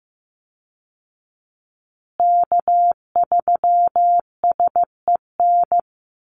Morse code, heard as a keyed tone, about 15 wpm, 700 Hz. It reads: K3SEN
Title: Call Signs Repeated 3x - 15wpm